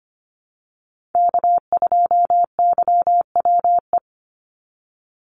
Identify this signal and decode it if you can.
Morse code, heard as a keyed tone, about 25 wpm, 700 Hz. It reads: K2YWE
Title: Call Signs Repeated 2x - 25wpm